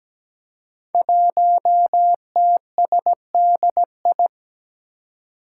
Morse code, heard as a keyed tone, about 17 wpm, 700 Hz. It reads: 1TSDI
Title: Five Challenging Letter-Number Combinations 17wpm